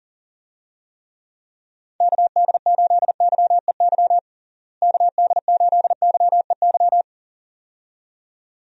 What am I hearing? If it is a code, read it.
Morse code, heard as a keyed tone, about 40 wpm, 700 Hz. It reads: KD8YEY KD8YEY
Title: USA Call Signs - 6 Characters - Repeated 2x - 40wpm